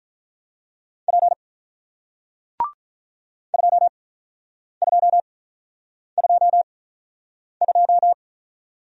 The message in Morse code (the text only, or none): P 2222